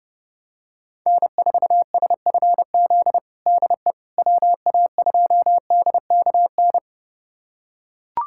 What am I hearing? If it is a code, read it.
Morse code, heard as a keyed tone, about 30 wpm, 700 Hz. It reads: N4SFZ DE WA2DKN
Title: USA Callsign DE Callsign 30wpm